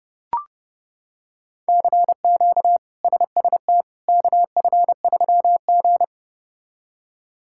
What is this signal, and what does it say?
Morse code, heard as a keyed tone, about 30 wpm, 700 Hz. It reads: CQ SST KF3G
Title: K1USN SST QSO Practice - 30wpm